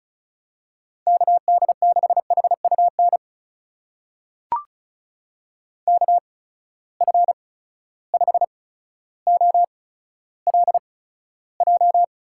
Morse code, heard as a keyed tone, about 36 wpm, spaced as for 15 wpm, 700 Hz. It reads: KD6HUN KF5OLJ
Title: USA Call Signs - 6 Characters - 8x Character Spacing - 35wpm